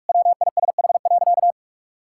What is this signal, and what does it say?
Morse code, heard as a keyed tone, about 45 wpm, 700 Hz. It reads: WISH.
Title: Sentences from Top 1600 Words 45wpm